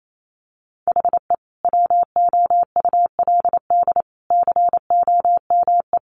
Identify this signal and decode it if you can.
Morse code, heard as a keyed tone, about 28 wpm, 700 Hz. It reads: HE WOULD COME
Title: Sentences from Top 1300 Words 28wpm